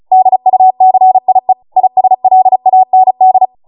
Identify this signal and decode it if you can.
Morse code, heard as a keyed tone, about 36 wpm, 750 Hz. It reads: DUCIE ISLAND